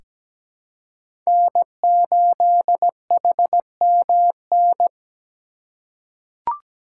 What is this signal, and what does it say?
Morse code, heard as a keyed tone, about 17 wpm, 700 Hz. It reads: N8HMN
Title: USA Call Signs - 5 Characters - 2x Character Spacing - 17wpm